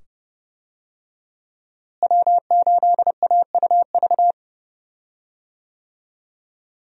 Morse code, heard as a keyed tone, about 30 wpm, 700 Hz. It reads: W8AUV
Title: USA Call Signs - 5 Characters - Repeated 2x - 30wpm